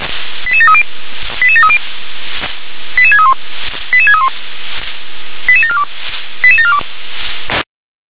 Five-tone selective call (ZVEI), heard as quick runs of five tones, tones of about 70 ms. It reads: 88730 88730 88631 88631 88742 88742